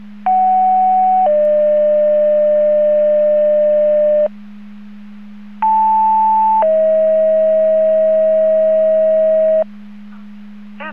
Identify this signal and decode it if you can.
Two-tone page, heard as a one-second tone, then a three-second tone, about 750 then 600 Hz, about 880 then 640 Hz. It is heard from 0.3 to 9.6 s.